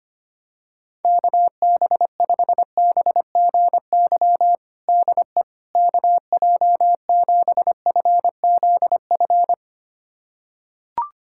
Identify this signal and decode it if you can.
Morse code, heard as a keyed tone, about 25 wpm, 700 Hz. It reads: KB5BGY DE KJ7FZF
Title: USA Callsign DE Callsign 25wpm